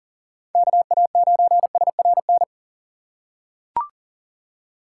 Morse code, heard as a keyed tone, about 38 wpm, 700 Hz. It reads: KA9SRN
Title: USA Call Signs - 6 Characters - 4x Character Spacing - 40wpm